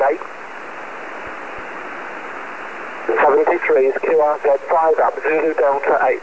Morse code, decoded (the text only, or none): none